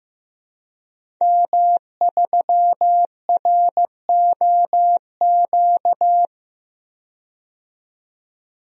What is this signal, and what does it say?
Morse code, heard as a keyed tone, about 15 wpm, 700 Hz. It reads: M3ROQ
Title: DXCC-100 Call Signs - 5 Characters - 15wpm